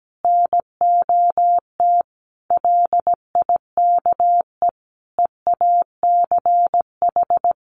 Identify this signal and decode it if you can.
Morse code, heard as a keyed tone, about 17 wpm, 700 Hz. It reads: NOT LIKE EACH
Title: Easy Phrases 17wpm